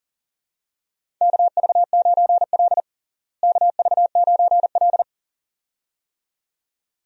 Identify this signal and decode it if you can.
Morse code, heard as a keyed tone, about 40 wpm, 700 Hz. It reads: KV9L KV9L
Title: USA Call Signs - 4 Characters - Repeated 2x - 40wpm